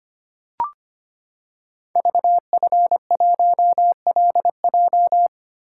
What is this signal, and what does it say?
Morse code, heard as a keyed tone, about 25 wpm, 700 Hz. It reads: VF1LJ
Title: Code Groups - 5 Letter:Numbers - 25wpm